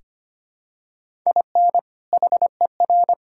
Morse code, heard as a keyed tone, about 25 wpm, 700 Hz. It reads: IN HER
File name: practice / 25wpm / Sentences with 2-Word Phrases - 25wpm